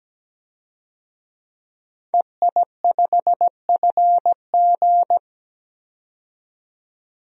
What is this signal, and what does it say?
Morse code, heard as a keyed tone, about 17 wpm, 700 Hz. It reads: EI5FG
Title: DXCC-100 Call Signs - 5 Characters - 17wpm